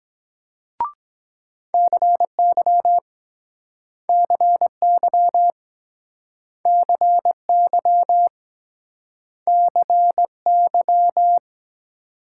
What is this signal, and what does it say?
Morse code, heard as a keyed tone, about 23 wpm, 700 Hz. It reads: CY CY CY CY